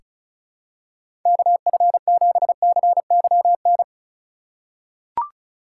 Morse code, heard as a keyed tone, about 34 wpm, 700 Hz. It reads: KF7CYN
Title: USA Call Signs - 6 Characters - 2x Character Spacing - 35wpm